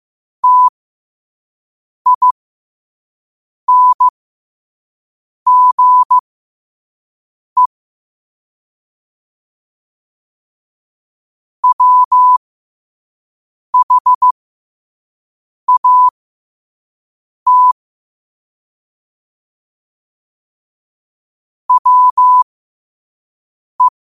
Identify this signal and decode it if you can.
Morse code, heard as a keyed tone, about 15 wpm, spaced as for 5 wpm, 1000 Hz. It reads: TINGE WHAT WE
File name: 5wpm